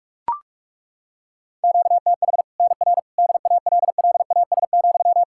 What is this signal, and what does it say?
Morse code, heard as a keyed tone, about 41 wpm, 700 Hz. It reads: QTH NR DALLAS,
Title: Practice QSO - Exchange 1 - Beginning - 45wpm